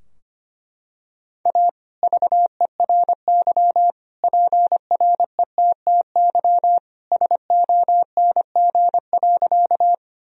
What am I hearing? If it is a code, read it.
Morse code, heard as a keyed tone, about 25 wpm, 700 Hz. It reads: A VERY PRETTY SONG.
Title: Sentences from Top 1000 Words - Review Part 2 - 25wpm